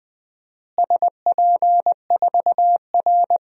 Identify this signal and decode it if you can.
Morse code, heard as a keyed tone, about 20 wpm, 700 Hz. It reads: SP4R
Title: DXCC-100 Call Signs - 4 Characters - Repeated 2x - 20wpm